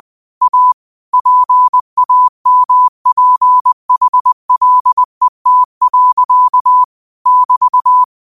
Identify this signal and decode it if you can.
Morse code, heard as a keyed tone, about 20 wpm, 1000 Hz. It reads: A PAMPHLET. <BT>
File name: qotdfiles / 20wpm